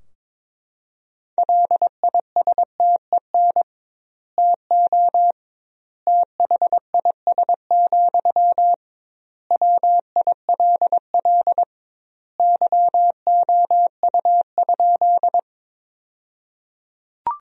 Morse code, heard as a keyed tone, about 22 wpm, 700 Hz. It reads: LISTEN TO THIS, WILL YOU?
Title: 2x Word-Spacing - Sentences from Top 400 Words - 22wpm